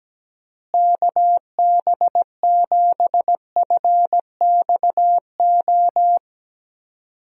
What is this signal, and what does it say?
Morse code, heard as a keyed tone, about 17 wpm, 700 Hz. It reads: KB7FXO
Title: USA Call Signs - 6 Characters - Repeated 2x - 17wpm